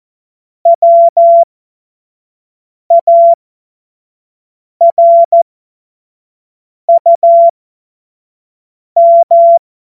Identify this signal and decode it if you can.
Morse code, heard as a keyed tone, about 14 wpm, spaced as for 5 wpm, 680 Hz. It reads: WARUM